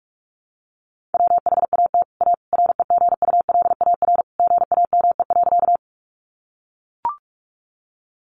Morse code, heard as a keyed tone, about 44 wpm, 700 Hz. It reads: WHAT A REGULAR GAME.
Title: Sentences from Top 900 Words 45wpm